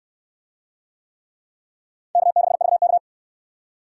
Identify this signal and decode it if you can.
Morse code, heard as a keyed tone, about 69 wpm, 700 Hz. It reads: K6VK